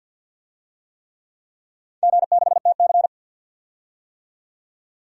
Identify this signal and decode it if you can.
Morse code, heard as a keyed tone, about 51 wpm, 700 Hz. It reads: G6TC
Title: DXCC-100 Call Signs - 4 Characters - 50wpm